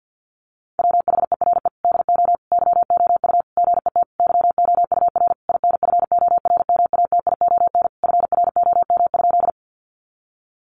Morse code, heard as a keyed tone, about 50 wpm, 700 Hz. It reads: WHERE DO YOU GET YOUR INFORMATION FROM?
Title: Sentences from Top 600 Words 50wpm